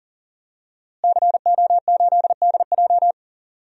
Morse code, heard as a keyed tone, about 40 wpm, 700 Hz. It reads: CO8DJ
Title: DXCC-100 Call Signs - 5 Characters - Repeated 3x - 40wpm